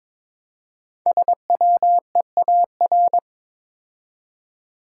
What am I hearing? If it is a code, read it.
Morse code, heard as a keyed tone, about 22 wpm, 700 Hz. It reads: SWEAR
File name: 5 Letter Words - First 3 Letters Given - 22wpm